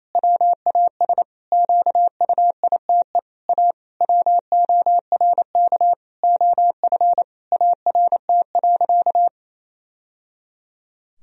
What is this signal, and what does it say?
Morse code, heard as a keyed tone, about 27 wpm, 700 Hz. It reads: WAS QUITE A WORK OF ART.
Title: Sentences from Top 1000 Words - Review Part 3 - 28wpm